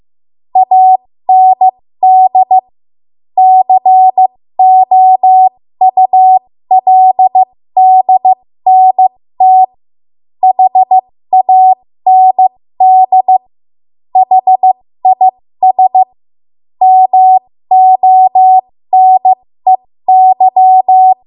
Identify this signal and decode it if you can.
Morse code, heard as a keyed tone, about 15 wpm, spaced as for 13 wpm, 750 Hz. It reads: AND COULDNT HAND HIS MONEY
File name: Morse / 13WPM